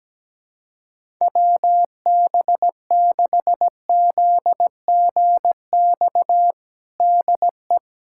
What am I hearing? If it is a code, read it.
Morse code, heard as a keyed tone, about 17 wpm, 700 Hz. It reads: WB6ZGX DE
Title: USA Callsign DE Callsign 17wpm